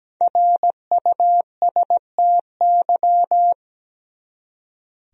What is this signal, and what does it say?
Morse code, heard as a keyed tone, about 17 wpm, 700 Hz. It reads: RUSTY
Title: USA Common Names - Max 5 Letters - 17wpm